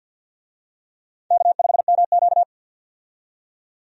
Morse code, heard as a keyed tone, about 52 wpm, 700 Hz. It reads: K5MQ
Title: USA Call Signs - 4 Characters - 50wpm